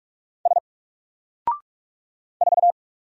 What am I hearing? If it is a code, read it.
Morse code, heard as a keyed tone, about 44 wpm, 700 Hz. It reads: S 4
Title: Single Character (Letter:Number:Punctuation:Prosign) Alt 45wpm